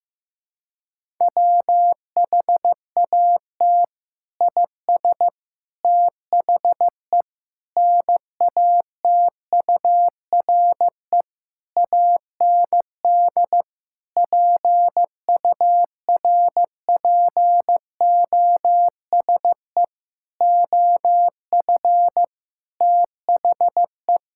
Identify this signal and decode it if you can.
Morse code, heard as a keyed tone, about 15 wpm, 700 Hz. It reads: WHAT IS THE NATURE AND PURPOSE OF THE
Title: Sentences from Top 1000 Words - Review Part 5 - 15wpm